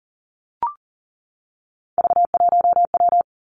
Morse code, heard as a keyed tone, about 41 wpm, 700 Hz. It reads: V1W
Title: Three Challenging Letter-Number Combinations 40wpm